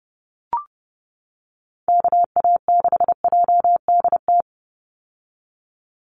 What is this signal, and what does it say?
Morse code, heard as a keyed tone, about 30 wpm, 700 Hz. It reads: KA6JDT
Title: USA Call Signs 30wpm